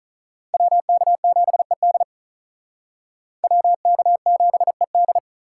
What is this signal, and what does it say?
Morse code, heard as a keyed tone, about 45 wpm, 700 Hz. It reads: WK7ED WK7ED